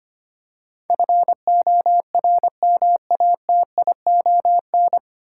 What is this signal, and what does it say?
Morse code, heard as a keyed tone, about 25 wpm, 700 Hz. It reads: FORMATION
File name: Nine Letter Words - 25wpm